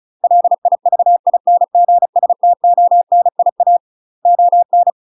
Morse code, heard as a keyed tone, about 34 wpm, 700 Hz. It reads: LIVINGSTONIA ON
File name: UK MASTER QSOs TRAINING/35WPM